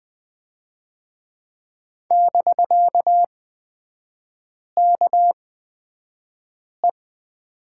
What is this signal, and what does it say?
Morse code, heard as a keyed tone, about 20 wpm, spaced as for 5 wpm, 700 Hz. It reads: <BK>KE
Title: 250 Lesson 40e - BK - Three Characters 20wpm